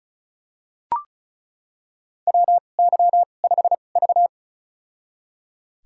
Morse code, heard as a keyed tone, about 34 wpm, spaced as for 25 wpm, 700 Hz. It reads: WY5V
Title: USA Call Signs - 4 Characters - 2x Character Spacing - 35wpm